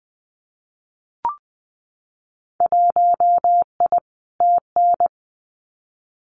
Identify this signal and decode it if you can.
Morse code, heard as a keyed tone, about 20 wpm, 700 Hz. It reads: 1I TN
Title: Winter Field Day Exchanges Encore 20wpm